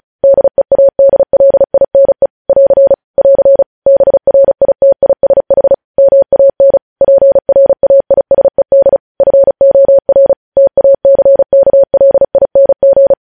Morse code, heard as a keyed tone, about 36 wpm, 550 Hz. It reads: DEADLINE <AR> <AR> BRITISH MAN PRAISED FOR TACKLING
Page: News headlines in Morse code at 35 wpm